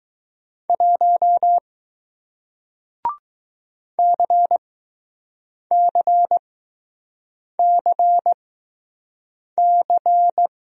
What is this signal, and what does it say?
Morse code, heard as a keyed tone, about 20 wpm, spaced as for 6 wpm, 700 Hz. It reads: 1 CCCC